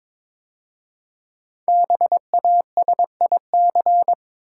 Morse code, heard as a keyed tone, about 22 wpm, 700 Hz. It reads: BASIC
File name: 5 Letter Words - First 3 Letters Given - 22wpm